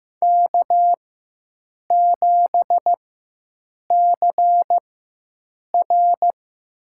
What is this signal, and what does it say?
Morse code, heard as a keyed tone, about 15 wpm, spaced as for 7 wpm, 700 Hz. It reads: K7CR
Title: USA Call Signs - 4 Characters - 4x Character Spacing - 15wpm